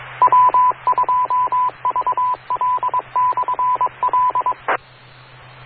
Morse code, heard as a keyed tone, about 22 wpm, 1000 Hz, with radio noise. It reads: W2VL/L